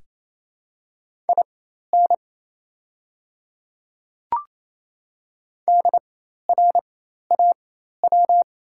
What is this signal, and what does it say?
Morse code, heard as a keyed tone, about 28 wpm, spaced as for 13 wpm, 690 Hz. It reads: IN DRAW
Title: 4x Character Spacing - Top 300 Words - 28wpm